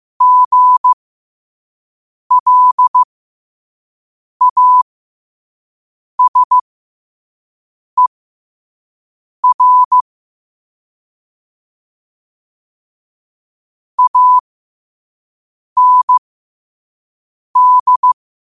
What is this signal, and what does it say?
Morse code, heard as a keyed tone, about 15 wpm, spaced as for 5 wpm, 1000 Hz. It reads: GLASER AND
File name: qotdfiles / 5wpm